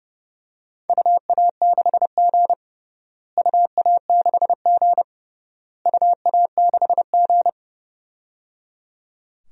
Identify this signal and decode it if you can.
Morse code, heard as a keyed tone, about 30 wpm, 700 Hz. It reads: UA6G UA6G UA6G
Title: DXCC-100 Call Signs - 4 Characters - Repeated 3x - 30wpm